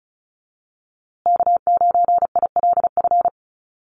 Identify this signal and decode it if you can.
Morse code, heard as a keyed tone, about 34 wpm, 700 Hz. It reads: K9ILF
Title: USA Call Signs Repeated 3x 35wpm